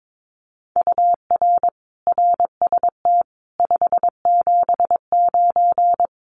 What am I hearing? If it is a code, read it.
Morse code, heard as a keyed tone, about 22 wpm, 700 Hz. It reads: UR RST 579
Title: Common QSO Phrases Encore 22wpm